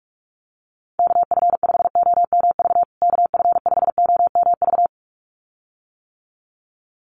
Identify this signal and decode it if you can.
Morse code, heard as a keyed tone, about 44 wpm, 700 Hz. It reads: KF5OMV KF5OMV
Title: USA Call Signs Repeated 2x 45wpm